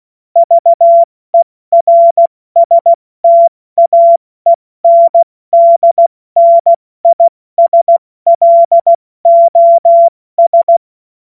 Morse code, heard as a keyed tone, about 16 wpm, 680 Hz. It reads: VERSTAENDNISLOS
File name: Morsekurs / Witze / Witz 19. April 2022 70 F80